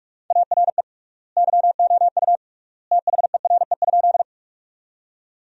Morse code, heard as a keyed tone, about 41 wpm, 700 Hz. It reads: ARE YOU THERE?
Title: 3x Word-Spacing - Sentences from Top 100 Words - 45wpm